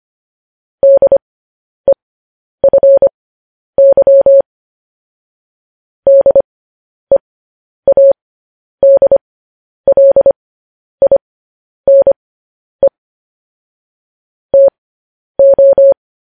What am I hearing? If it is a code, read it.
Morse code, heard as a keyed tone, about 25 wpm, spaced as for 10 wpm, 550 Hz. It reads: DEFY DEADLINE TO
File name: podcast / News headlines-1x-25x10